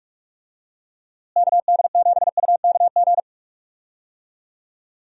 Morse code, heard as a keyed tone, about 46 wpm, 700 Hz. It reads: KD7UKG
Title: USA Call Signs - 6 Characters - Repeated 2x - 45wpm